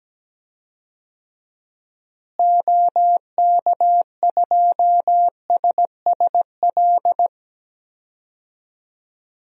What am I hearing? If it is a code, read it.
Morse code, heard as a keyed tone, about 17 wpm, 700 Hz. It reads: OK2SSL
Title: DXCC-100 Call Signs - 6 Characters - Repeated 2x - 17wpm